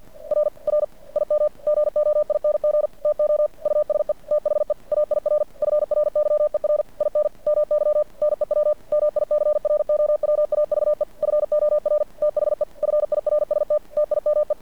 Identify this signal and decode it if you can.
Morse code, heard as a keyed tone, about 50 wpm, 590 Hz. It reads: R R IM GOING TO USE THE AIR FRYER IN MY NEW MICROWAVE FOR THE FIRST TIME
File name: archived / CRSnet